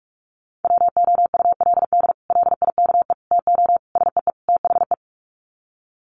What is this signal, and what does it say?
Morse code, heard as a keyed tone, about 44 wpm, 690 Hz. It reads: WOULD LIKE TO SEE THE
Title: Sets of 5 Words Encore2 45wpm